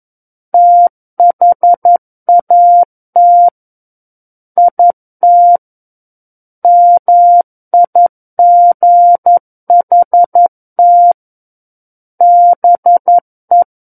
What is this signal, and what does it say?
Morse code, heard as a keyed tone, about 11 wpm, 700 Hz. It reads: THAT IT MIGHT BE
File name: Book One / 02-Chapter 2 - 11 WPM